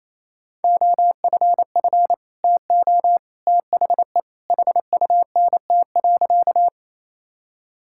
Morse code, heard as a keyed tone, about 28 wpm, 700 Hz. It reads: OFF TO THE HUNT.
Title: Sentences from Top 1600 Words 28wpm